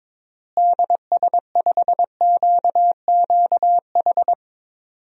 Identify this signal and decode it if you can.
Morse code, heard as a keyed tone, about 22 wpm, 700 Hz. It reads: DS5QQH
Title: DXCC-100 Call Signs - 6 Characters - Repeated 3x - 22wpm